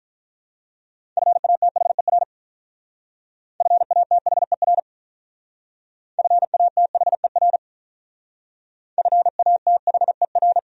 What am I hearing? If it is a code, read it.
Morse code, heard as a keyed tone, about 43 wpm, 700 Hz. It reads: FATHER FATHER FATHER FATHER
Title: Speed-Racing Top 300 Words 35wpm